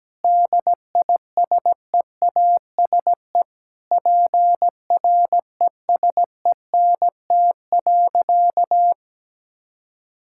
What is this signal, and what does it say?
Morse code, heard as a keyed tone, about 17 wpm, 700 Hz. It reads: DISEASE PRESENT.
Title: Sentences from Top 1000 Words - Review Part 6 - 17wpm